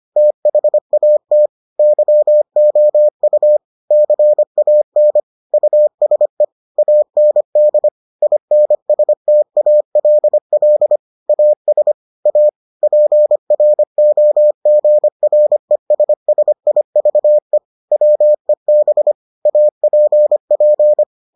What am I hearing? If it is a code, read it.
Morse code, heard as a keyed tone, about 25 wpm, 600 Hz. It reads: THAT YOU CAN USE AND INSTALL AS A PROGRESSIVE WEB APP